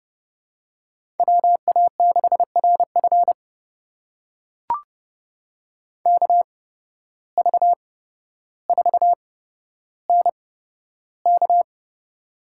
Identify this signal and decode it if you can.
Morse code, heard as a keyed tone, about 30 wpm, spaced as for 13 wpm, 700 Hz. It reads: WA6RF KV4NK
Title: USA Call Signs - 5 Characters - 8x Character Spacing - 30wpm